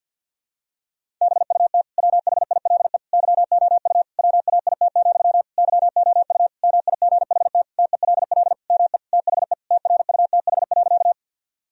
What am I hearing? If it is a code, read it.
Morse code, heard as a keyed tone, about 52 wpm, 700 Hz. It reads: BUT WHILE YOU WAIT, YOU MIGHT TELL ME THE TRUTH.
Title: Sentences from Top 1000 Words - Review Part 1 - 50wpm